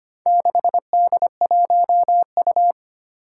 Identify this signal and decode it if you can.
Morse code, heard as a keyed tone, about 25 wpm, 700 Hz. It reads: 6D1U
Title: Four Challenging Letter-Number Combinations 25wpm